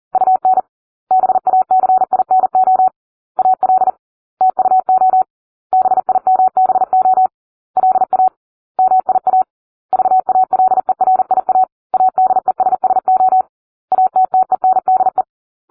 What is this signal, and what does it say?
Morse code, heard as a keyed tone, about 40 wpm, 750 Hz. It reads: UN BACINO AL TUO BIMBO LA MIA VALERIA ADESSO ATTENDE